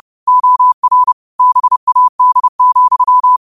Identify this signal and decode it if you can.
Morse code, heard as a keyed tone, about 30 wpm, 1000 Hz. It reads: OR BAD,